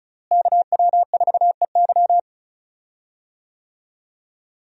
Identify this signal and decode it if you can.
Morse code, heard as a keyed tone, about 34 wpm, 700 Hz. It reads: KW4EY
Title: USA Call Signs - 5 Characters - Repeated 3x - 35wpm